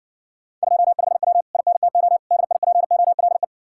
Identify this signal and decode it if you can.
Morse code, heard as a keyed tone, about 61 wpm, 700 Hz. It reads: 25W INTO DIPOLE